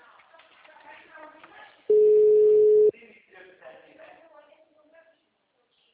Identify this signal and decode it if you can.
Ringback tone (Europe, 425 Hz), with speech.
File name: out- -2004- - - .18203